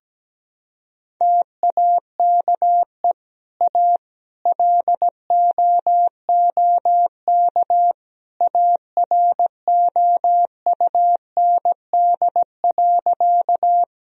Morse code, heard as a keyed tone, about 17 wpm, 700 Hz. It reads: TAKE A LOOK AROUND.
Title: Sentences from Top 1000 Words - Review Part 3 - 17wpm